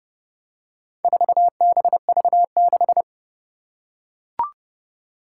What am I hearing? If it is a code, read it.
Morse code, heard as a keyed tone, about 30 wpm, 700 Hz. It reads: 4BV6
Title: Four Challenging Letter-Number Combinations 30wpm